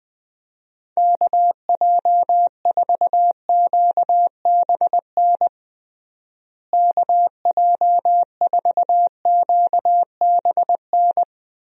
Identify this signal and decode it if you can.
Morse code, heard as a keyed tone, about 20 wpm, 700 Hz. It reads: KJ4QBN KJ4QBN
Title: USA Call Signs - 6 Characters - Repeated 2x - 20wpm